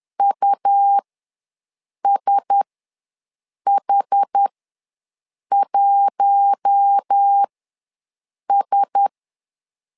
Morse code, heard as a keyed tone, about 10 wpm, spaced as for 6 wpm, 800 Hz. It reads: USH1S
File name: LEZIONI / 4 Q1S ws 10wpm 2208